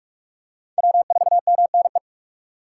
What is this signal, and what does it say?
Morse code, heard as a keyed tone, about 44 wpm, 700 Hz. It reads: W4MNE